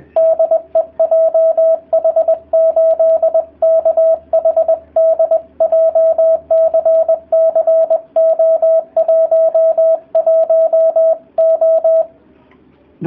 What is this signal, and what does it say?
Morse code, heard as a keyed tone, about 20 wpm, 650 Hz, with radio noise. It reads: DE JH8KHD JCC O11O